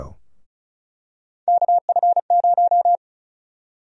Morse code, heard as a keyed tone, about 36 wpm, 700 Hz. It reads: KF0
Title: USA Call Sign Prefixes - 35wpm